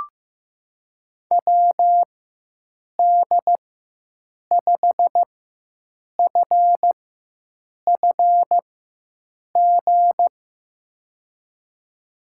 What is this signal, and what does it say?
Morse code, heard as a keyed tone, about 15 wpm, spaced as for 7 wpm, 700 Hz. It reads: WD5FFG
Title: USA Call Signs - 6 Characters - 4x Character Spacing - 15wpm